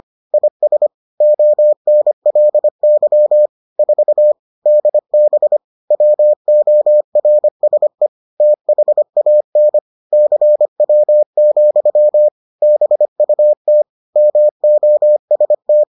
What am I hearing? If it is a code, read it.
Morse code, heard as a keyed tone, about 25 wpm, 600 Hz. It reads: IS ONLY 4 DB WORSE THAN CW, BUT MOST